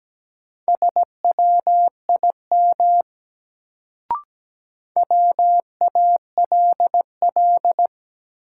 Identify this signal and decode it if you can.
Morse code, heard as a keyed tone, about 17 wpm, 700 Hz. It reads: SWIM WALL
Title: Sentences from Top 1000 Words - Review Part 6 - New Word Drill - 17wpm